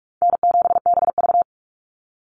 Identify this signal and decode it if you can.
Morse code, heard as a keyed tone, about 44 wpm, 700 Hz. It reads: N7BV
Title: Call Signs - 45wpm